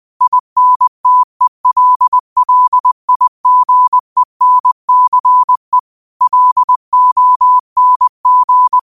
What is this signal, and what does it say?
Morse code, heard as a keyed tone, about 20 wpm, 1000 Hz. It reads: INTELLIGENCE LONG
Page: Daily quotes in Morse Code at 20 Words per Minute. Use this to improve your code copying skill.